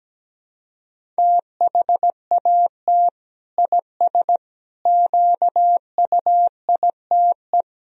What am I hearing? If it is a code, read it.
Morse code, heard as a keyed tone, about 17 wpm, 700 Hz. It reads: THAT IS QUITE
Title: Sentences from Top 1000 Words - Review Part 2 - 17wpm